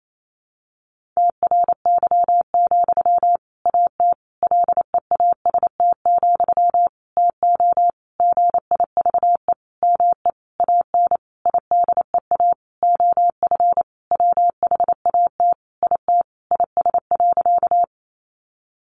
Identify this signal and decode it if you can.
Morse code, heard as a keyed tone, about 28 wpm, 700 Hz. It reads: TRY, AT LEAST, TO GIVE ME AN IDEA OF WHAT IT IS.
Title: Sentences from Top 700 Words 28wpm